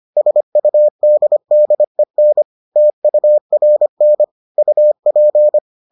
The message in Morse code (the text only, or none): SUDDEN TURN UP